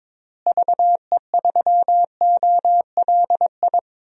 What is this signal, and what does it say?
Morse code, heard as a keyed tone, about 21 wpm, 700 Hz. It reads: VE3OLI
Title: DXCC-100 Call Signs - 6 Characters - Repeated 2x - 22wpm